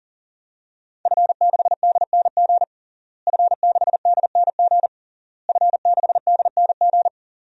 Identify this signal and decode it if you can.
Morse code, heard as a keyed tone, about 37 wpm, 700 Hz. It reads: F6DNG F6DNG F6DNG
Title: DXCC-100 Call Signs - 5 Characters - Repeated 3x - 40wpm